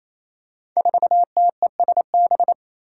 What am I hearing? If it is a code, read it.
Morse code, heard as a keyed tone, about 28 wpm, 700 Hz. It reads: 4TESB
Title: Five Challenging Letter-Number Combinations 28wpm